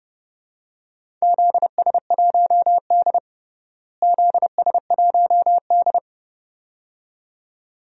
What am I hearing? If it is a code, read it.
Morse code, heard as a keyed tone, about 30 wpm, 700 Hz. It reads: ZS1D ZS1D
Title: DXCC-100 Call Signs - 4 Characters - Repeated 2x - 30wpm